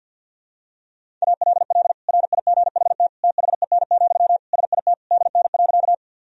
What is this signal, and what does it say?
Morse code, heard as a keyed tone, about 46 wpm, 700 Hz. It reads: ALL RIGHT THEN, SIT DN.
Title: Sentences from Top 500 Words and Abbreviations 50wpm